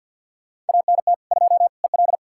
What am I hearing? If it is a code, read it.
Morse code, heard as a keyed tone, about 52 wpm, 700 Hz. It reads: ANT 2 EL
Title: Practice QSO - Exchange 2 - Info - 50wpm